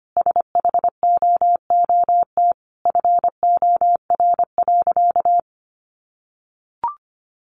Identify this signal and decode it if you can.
Morse code, heard as a keyed tone, about 25 wpm, 700 Hz. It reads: SHOOT FOR.
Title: Sentences from Top 1400 Words 25wpm